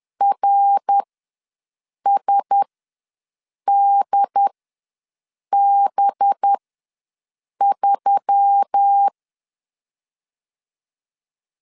Morse code, heard as a keyed tone, about 10 wpm, spaced as for 6 wpm, 800 Hz. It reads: RSDB3
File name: LEZIONI / 5 LBVC / 5 LBV ws 10wpm 3645